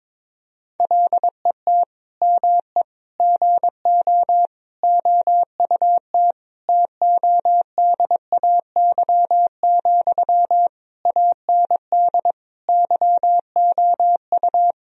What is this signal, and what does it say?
Morse code, heard as a keyed tone, about 22 wpm, 700 Hz. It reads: LET ME GO OUT TODAY, AND YOU
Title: Sentences from Top 1000 Words - Review Part 4 - 22wpm